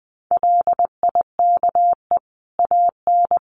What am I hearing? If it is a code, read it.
Morse code, heard as a keyed tone, about 20 wpm, 700 Hz. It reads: LIKE AN
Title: Sentences from Top 1500 Words 20wpm